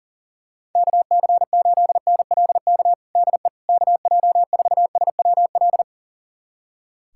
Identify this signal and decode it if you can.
Morse code, heard as a keyed tone, about 40 wpm, 700 Hz. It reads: KC8NLK DE KJ4SWL
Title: USA Callsign DE Callsign 40wpm